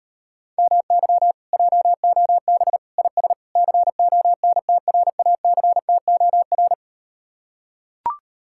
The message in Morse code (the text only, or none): MY JOB IS CONTRACTOR